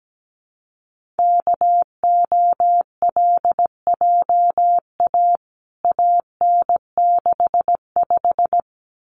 Morse code, heard as a keyed tone, about 17 wpm, 700 Hz. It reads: KOLJA AN65
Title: CWT Exchange 17wpm